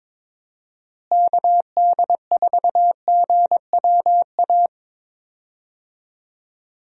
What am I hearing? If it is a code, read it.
Morse code, heard as a keyed tone, about 22 wpm, 700 Hz. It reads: KD4GWA
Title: USA Call Signs - 6 Characters - Repeated 2x - 22wpm